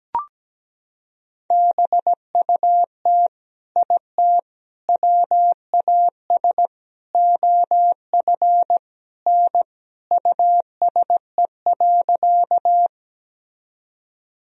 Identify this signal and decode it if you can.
Morse code, heard as a keyed tone, about 17 wpm, 700 Hz. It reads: BUT IT WAS OF N USE.
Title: Sentences from Top 100 Words and Abbreviations 17wpm